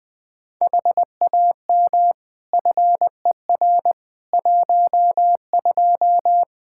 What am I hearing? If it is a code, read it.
Morse code, heard as a keyed tone, about 20 wpm, 700 Hz. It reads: HAM FER 12
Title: Practice QSO - Exchange 3 - More Info - 20wpm